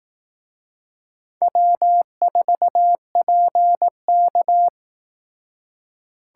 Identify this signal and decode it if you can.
Morse code, heard as a keyed tone, about 18 wpm, 700 Hz. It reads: W4PK
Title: Speed Racing - USA 1x2 Call Signs 15wpm